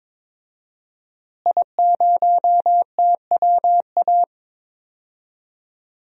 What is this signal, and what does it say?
Morse code, heard as a keyed tone, about 22 wpm, 700 Hz. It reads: I0TWA
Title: DXCC-100 Call Signs - 5 Characters - Repeated 3x - 22wpm